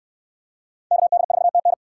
Morse code, heard as a keyed tone, about 67 wpm, 700 Hz. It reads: KD4TN